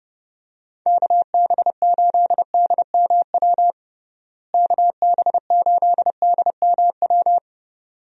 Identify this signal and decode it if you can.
Morse code, heard as a keyed tone, about 32 wpm, 700 Hz. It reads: KB8DMW KB8DMW
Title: USA Call Signs - 6 Characters - Repeated 2x - 30wpm